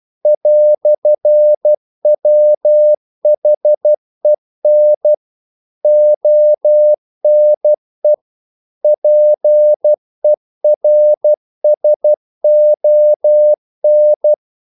Morse code, heard as a keyed tone, about 12 wpm, 600 Hz. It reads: "WHEN ONE PERSON
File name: CW / Practice / EnglishQuotations0112